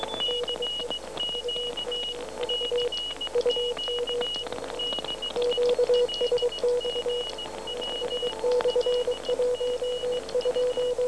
Morse code, heard as a keyed tone, about 17 wpm, 2900 Hz, with noise. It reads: CQ CQ DE LGW LGU
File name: UMV 12730 1839